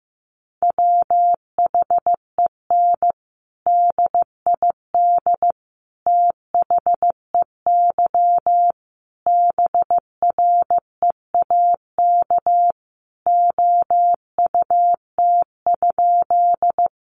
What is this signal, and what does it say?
Morse code, heard as a keyed tone, about 15 wpm, 700 Hz. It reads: WHEN DID THEY BREAK OUT?
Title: Sentences from Top 900 Words 15wpm